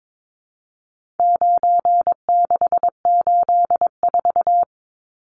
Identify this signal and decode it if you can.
Morse code, heard as a keyed tone, about 22 wpm, 700 Hz. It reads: 9684
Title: Numbers - 4-Digit 22wpm